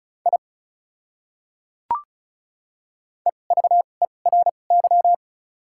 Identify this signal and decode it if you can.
Morse code, heard as a keyed tone, about 36 wpm, spaced as for 25 wpm, 700 Hz. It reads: I EVERY
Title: 2x Character Spacing - Top 200 Words - 35wpm